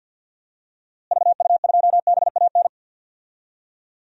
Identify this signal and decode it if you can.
Morse code, heard as a keyed tone, about 51 wpm, 700 Hz. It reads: VU2BAN
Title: DXCC-100 Call Signs - 6 Characters - Repeated 3x - 50wpm